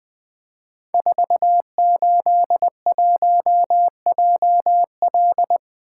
Morse code, heard as a keyed tone, about 20 wpm, 700 Hz. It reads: 481JL
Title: Code Groups - 5 Letter:Numbers - 20wpm